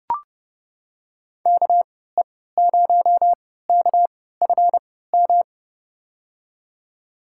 Morse code, heard as a keyed tone, about 30 wpm, spaced as for 17 wpm, 700 Hz. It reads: KE0KFM
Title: USA Call Signs - 6 Characters - 3x Character Spacing - 30wpm